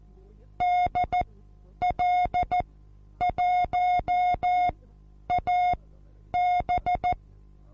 Morse code, heard as a keyed tone, about 14 wpm, spaced as for 9 wpm, 720 Hz. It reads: DL1AB
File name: Onlinelehrgang / morsen / DJ4UF Morsen23